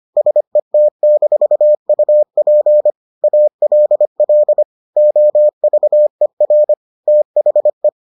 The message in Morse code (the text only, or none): SET-UP ALL OVER THE